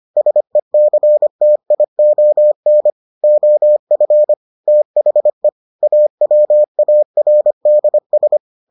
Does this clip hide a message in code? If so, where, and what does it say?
Yes: Morse code from 0.2 to 8.4 s, heard as a keyed tone, about 25 wpm, 600 Hz. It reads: SECTION OF THE AWARDS